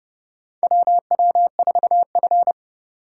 Morse code, heard as a keyed tone, about 30 wpm, 700 Hz. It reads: WW4F
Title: USA Call Signs - 3x Character Spacing - 30wpm